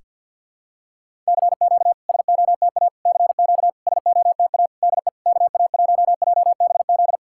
Morse code, heard as a keyed tone, about 50 wpm, 660 Hz. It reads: CQ SOTA CQ SOTA DE KA1JBZ
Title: SOTA QSO Practice 50wpm